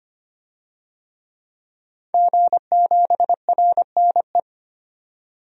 Morse code, heard as a keyed tone, about 25 wpm, 700 Hz. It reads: G7RNE
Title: DXCC-100 Call Signs - 5 Characters - 25wpm